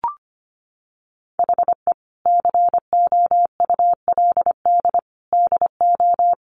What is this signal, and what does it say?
Morse code, heard as a keyed tone, about 25 wpm, 700 Hz. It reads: HE COULD DO
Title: Sentences from Top 1500 Words 25wpm